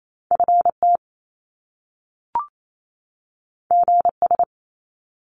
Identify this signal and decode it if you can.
Morse code, heard as a keyed tone, about 28 wpm, 700 Hz. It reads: FT GS
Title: Top Two Letter Combinations in Words 28wpm